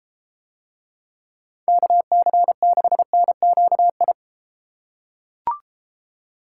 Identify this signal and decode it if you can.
Morse code, heard as a keyed tone, about 31 wpm, 700 Hz. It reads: KC6NQI